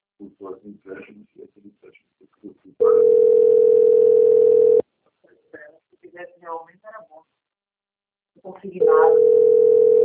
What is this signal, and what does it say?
Ringback tone (North America), 440 + 480 Hz, with speech.
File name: rg-1600-unknown- - - .13667